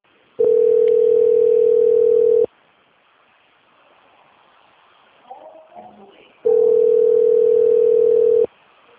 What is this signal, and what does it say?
Ringback tone (North America), 440 + 480 Hz, with speech.
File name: out- - - - - .44114